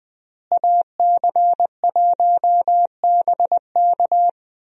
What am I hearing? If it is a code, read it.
Morse code, heard as a keyed tone, about 20 wpm, 700 Hz. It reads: AC1BK
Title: USA Call Signs - 5 Characters - Repeated 3x - 20wpm